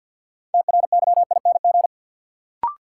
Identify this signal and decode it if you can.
Morse code, heard as a keyed tone, about 53 wpm, 700 Hz. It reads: TRYING
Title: Sets of 2 Words - New Word Drill - 50wpm